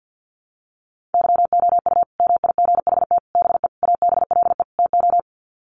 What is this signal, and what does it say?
Morse code, heard as a keyed tone, about 50 wpm, 700 Hz. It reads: YOU MIGHT BE ABLE TO
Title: Sets of 5 Words Encore2 50wpm